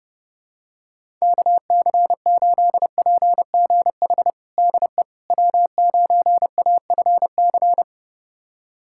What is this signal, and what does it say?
Morse code, heard as a keyed tone, about 32 wpm, 700 Hz. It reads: KC8PGH DE W9AFC
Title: USA Callsign DE Callsign 30wpm